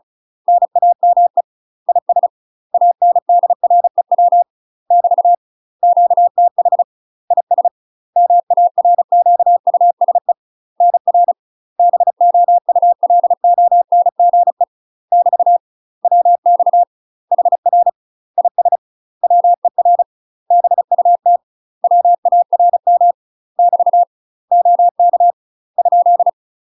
Morse code, heard as a keyed tone, about 36 wpm, 700 Hz. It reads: NAME IS ANDREW <BT> QTH IS MARQUSE NR BOULONGE <BT> WX HR IS WER BUT WARM <BT> OK ?
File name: UK MASTER QSOs TRAINING / 35WPM / UK MASTER QSOs 0031